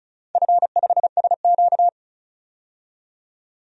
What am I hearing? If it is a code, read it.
Morse code, heard as a keyed tone, about 36 wpm, 700 Hz. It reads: F5SQ
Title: DXCC-100 Call Signs - 4 Characters - 35wpm